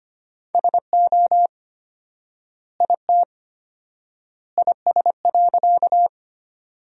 Morse code, heard as a keyed tone, about 25 wpm, 700 Hz. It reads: SO IT IS.
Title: 4x Word-Spacing - Sentences from Top 100 Words - 25wpm